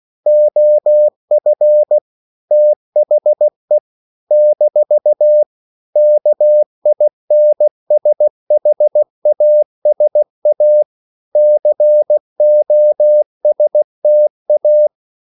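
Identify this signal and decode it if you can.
Morse code, heard as a keyed tone, about 16 wpm, 600 Hz. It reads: OF THE - KINSHASA COSTA
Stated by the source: CW practice of national capitals at 16 words per minute, from ON6ZQ.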